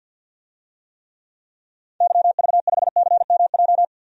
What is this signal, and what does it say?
Morse code, heard as a keyed tone, about 51 wpm, 700 Hz. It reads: YV5CMJ
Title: DXCC-100 Call Signs - 6 Characters - 50wpm